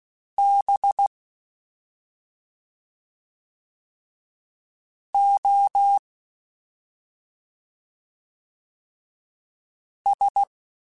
Morse code, heard as a keyed tone, about 16 wpm, 800 Hz. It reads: B O S
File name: Special INGs-mp3s / Special INGs-mp3s / Slow ing @10